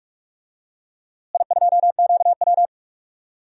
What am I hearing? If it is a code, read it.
Morse code, heard as a keyed tone, about 46 wpm, 700 Hz. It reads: I2QW
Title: DXCC-100 Call Signs - 4 Characters - Repeated 3x - 45wpm